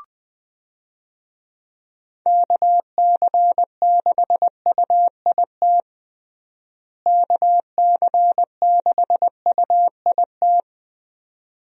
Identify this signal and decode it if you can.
Morse code, heard as a keyed tone, about 19 wpm, 700 Hz. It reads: KC6UIT KC6UIT
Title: USA Call Signs - 6 Characters - Repeated 2x - 20wpm